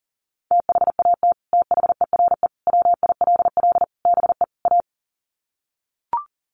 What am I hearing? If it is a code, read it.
Morse code, heard as a keyed tone, about 37 wpm, 700 Hz. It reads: THAT THERE WILL BE A
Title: Sets of 5 Words Encore2 40wpm